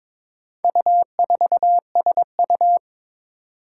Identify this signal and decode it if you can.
Morse code, heard as a keyed tone, about 23 wpm, 700 Hz. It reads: U4SU
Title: Four Challenging Letter-Number Combinations 22wpm